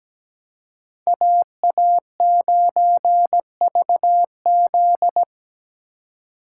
Morse code, heard as a keyed tone, about 17 wpm, 700 Hz. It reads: AA9VZ
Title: USA Call Signs - 5 Characters - 4x Character Spacing - 17wpm